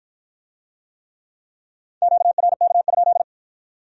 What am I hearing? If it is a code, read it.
Morse code, heard as a keyed tone, about 53 wpm, 700 Hz. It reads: QRK?